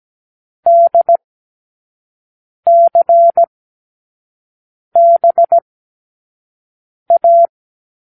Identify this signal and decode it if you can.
Morse code, heard as a keyed tone, about 17 wpm, spaced as for 5 wpm, 680 Hz. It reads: DCBA
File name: k7qo / Lesson6(A-D)